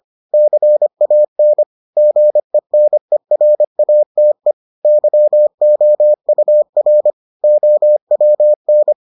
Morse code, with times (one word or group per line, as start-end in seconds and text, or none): 0.3-1.6 CAN
2.0-4.5 GENERATE
4.8-7.1 YOUR
7.4-8.9 OWN